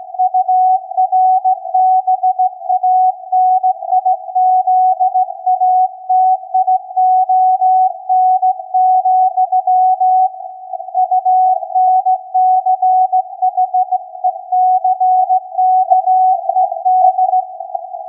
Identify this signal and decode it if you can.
Morse code, heard as a keyed tone, about 17 wpm, 740 Hz. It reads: URBANIZATION, UNCHECKED
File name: iz7slz / VLF / SAQ 16-11-2022